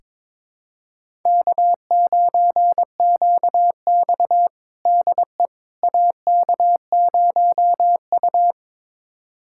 Morse code, heard as a keyed tone, about 22 wpm, 700 Hz. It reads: K9QX DE AK0U
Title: USA Callsign DE Callsign - 4 Characters - 22wpm